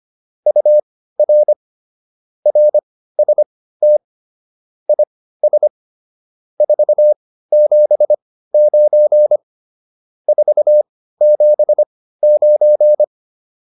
Morse code, heard as a keyed tone, about 25 wpm, spaced as for 15 wpm, 600 Hz. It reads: UR RST IS 479 479